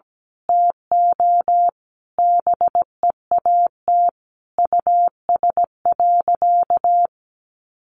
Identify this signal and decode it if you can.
Morse code, heard as a keyed tone, about 17 wpm, 700 Hz. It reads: TO BEAT US.
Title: Sentences from Top 1300 Words 17wpm